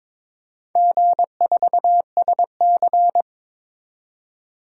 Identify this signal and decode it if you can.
Morse code, heard as a keyed tone, about 22 wpm, 700 Hz. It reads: G4SC
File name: practice / 22wpm / Four Letter-Number Combinations - 22wpm